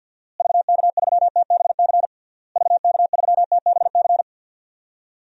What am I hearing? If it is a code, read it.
Morse code, heard as a keyed tone, about 48 wpm, 700 Hz. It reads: VK3TBC VK3TBC
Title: DXCC-100 Call Signs - 6 Characters - Repeated 3x - 50wpm